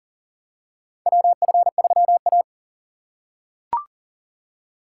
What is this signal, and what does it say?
Morse code, heard as a keyed tone, about 40 wpm, 700 Hz. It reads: WF3A
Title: USA Call Signs - 4 Characters - 4x Character Spacing - 40wpm